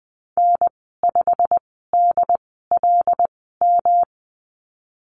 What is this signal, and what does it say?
Morse code, heard as a keyed tone, about 20 wpm, spaced as for 15 wpm, 700 Hz. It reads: N5DLM
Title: 161 Lesson 26g - Letter g - Call Signs 20wpm